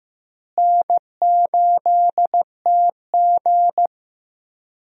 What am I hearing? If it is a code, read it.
Morse code, heard as a keyed tone, about 15 wpm, 700 Hz. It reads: N8TG
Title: USA Call Signs - 4 Characters - Repeated 3x - 15wpm